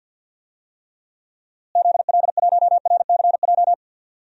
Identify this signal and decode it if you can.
Morse code, heard as a keyed tone, about 50 wpm, 700 Hz. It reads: ZL1RCJ
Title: DXCC-100 Call Signs - 6 Characters - Repeated 2x - 50wpm